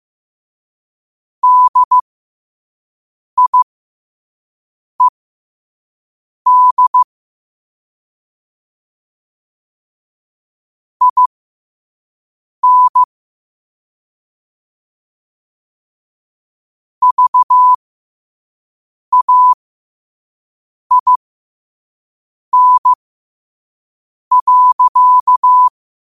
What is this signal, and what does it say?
Morse code, heard as a keyed tone, about 15 wpm, spaced as for 5 wpm, 1000 Hz. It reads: DIED IN VAIN.
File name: qotdfiles / 5wpm